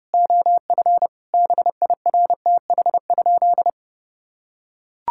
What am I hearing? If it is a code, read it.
Morse code, heard as a keyed tone, about 30 wpm, 700 Hz. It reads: OF BIRTH?
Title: Sentences from Top 1000 Words - Review Part 6 - 30wpm